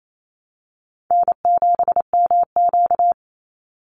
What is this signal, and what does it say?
Morse code, heard as a keyed tone, about 28 wpm, 690 Hz. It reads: N7MQ
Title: Call Signs Repeated 3x - 28wpm